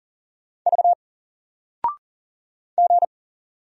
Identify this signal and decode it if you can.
Morse code, heard as a keyed tone, about 40 wpm, 700 Hz. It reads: V G
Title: Single Letter-Number 40wpm